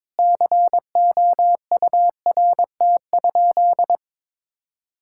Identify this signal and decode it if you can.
Morse code, heard as a keyed tone, about 21 wpm, 700 Hz. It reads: COURT?
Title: Sentences from Top 1600 Words 22wpm